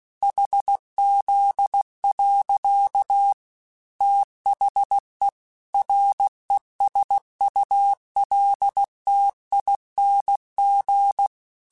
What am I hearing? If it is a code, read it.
Morse code, heard as a keyed tone, about 16 wpm, 800 Hz. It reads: HZ. THE RESULTING
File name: Technical Subjects / Technical Subjects-mp3s / XL @15